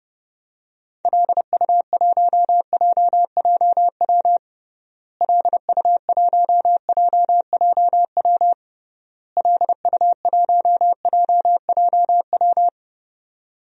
Morse code, heard as a keyed tone, about 30 wpm, 700 Hz. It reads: LU1JJW LU1JJW LU1JJW
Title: DXCC-100 Call Signs - 6 Characters - Repeated 3x - 30wpm